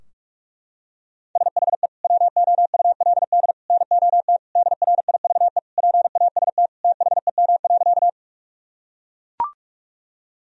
Morse code, heard as a keyed tone, about 41 wpm, 700 Hz. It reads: SHE WOULD NOT DRIVE PAST THEM.
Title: Sentences from Top 1000 Words - Review Part 3 - 45wpm